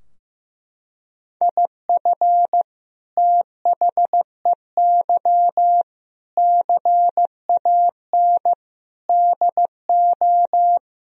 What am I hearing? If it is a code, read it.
Morse code, heard as a keyed tone, about 15 wpm, 700 Hz. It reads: IF THEY CAN DO
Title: Sentences from Top 1000 Words - Review Part 4 - 15wpm